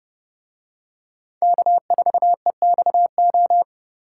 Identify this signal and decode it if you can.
Morse code, heard as a keyed tone, about 31 wpm, 700 Hz. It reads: K4EXO